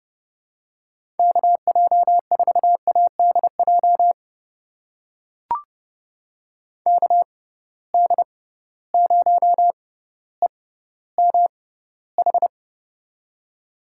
Morse code, heard as a keyed tone, about 30 wpm, spaced as for 15 wpm, 700 Hz. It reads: KJ4ADJ KD0EMH
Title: USA Call Signs - 6 Characters - 6x Character Spacing - 30wpm